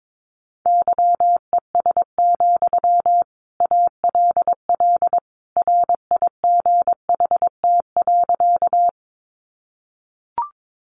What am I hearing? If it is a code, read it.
Morse code, heard as a keyed tone, about 22 wpm, 700 Hz. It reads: YES, ALL RIGHT.
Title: Sentences from Top 500 Words 22wpm